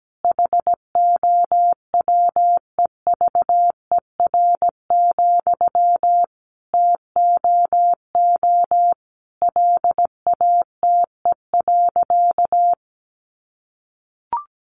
Morse code, heard as a keyed tone, about 17 wpm, 700 Hz. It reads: HOWEVER, TOO LATE.
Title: Sentences from Top 600 Words 17wpm